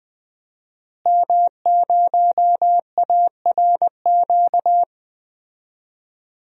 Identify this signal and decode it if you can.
Morse code, heard as a keyed tone, about 20 wpm, 700 Hz. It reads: M0ARQ
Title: DXCC-100 Call Signs - 5 Characters - Repeated 2x - 20wpm